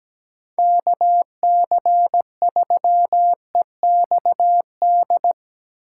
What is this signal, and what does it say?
Morse code, heard as a keyed tone, about 17 wpm, 700 Hz. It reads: KC3EXD
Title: USA Call Signs - 6 Characters - 4x Character Spacing - 17wpm